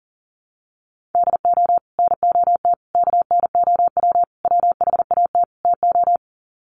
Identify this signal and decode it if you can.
Morse code, heard as a keyed tone, about 40 wpm, 700 Hz. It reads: DO NOT KNOW WHAT TO
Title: Sets of 5 Words 40wpm